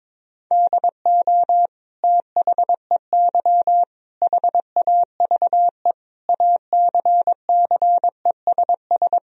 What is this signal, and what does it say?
Morse code, heard as a keyed tone, about 22 wpm, 700 Hz. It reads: DO THEY HAVE ACCESS
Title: Sentences from Top 1000 Words - Review Part 4 - 22wpm